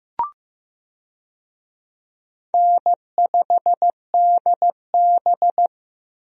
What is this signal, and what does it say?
Morse code, heard as a keyed tone, about 15 wpm, 700 Hz. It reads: N5DB
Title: USA Call Signs - 4 Characters - Repeated 3x - 15wpm